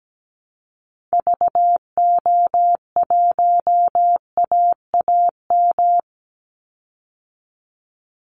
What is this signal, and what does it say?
Morse code, heard as a keyed tone, about 17 wpm, 700 Hz. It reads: VO1AAM
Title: Call Signs Repeated 2x - 17wpm